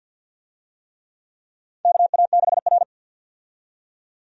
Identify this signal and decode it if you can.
Morse code, heard as a keyed tone, about 53 wpm, 700 Hz. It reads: KA6R
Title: USA Call Signs - 4 Characters - 50wpm